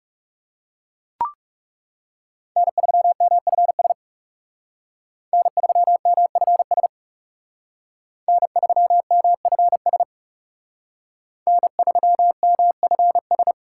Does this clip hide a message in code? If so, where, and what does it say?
Yes: Morse code from 2.6 to 13.5 s, heard as a keyed tone, about 38 wpm, 700 Hz. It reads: N3MFS N3MFS N3MFS N3MFS